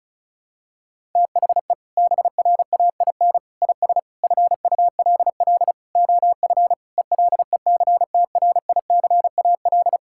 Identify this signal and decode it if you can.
Morse code, heard as a keyed tone, about 34 wpm, 700 Hz. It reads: THE BRAIN IS FULL OF ELECTRICAL
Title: Sentences from Top 1000 Words - Review Part 6 - 35wpm